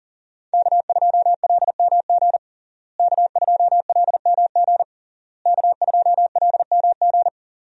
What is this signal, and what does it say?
Morse code, heard as a keyed tone, about 40 wpm, 700 Hz. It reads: K2LMG K2LMG K2LMG
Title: USA Call Signs - 5 Characters - Repeated 3x - 40wpm